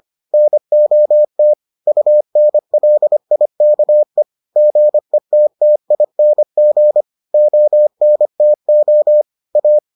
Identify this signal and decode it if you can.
Morse code, heard as a keyed tone, about 25 wpm, 600 Hz. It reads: NOT UNLIKE GETTING ONTO A